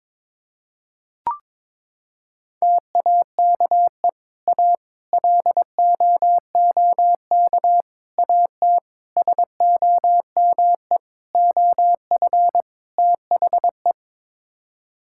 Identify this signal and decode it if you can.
Morse code, heard as a keyed tone, about 22 wpm, 700 Hz. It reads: TAKE A LOOK AT SOME OF THE
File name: practice / 22wpm / Sets of 7 Words - 22wpm